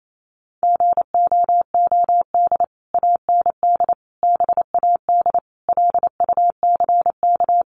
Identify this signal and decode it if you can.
Morse code, heard as a keyed tone, about 28 wpm, 700 Hz. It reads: GOOD AND BAD LUCK
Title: Sentences from Top 1500 Words 28wpm